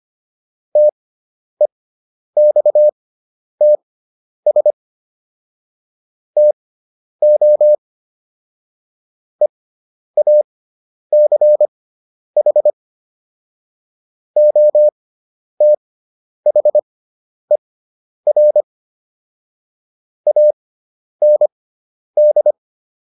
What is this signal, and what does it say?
Morse code, heard as a keyed tone, about 25 wpm, spaced as for 10 wpm, 600 Hz. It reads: TEXTS TO EACH OTHER AND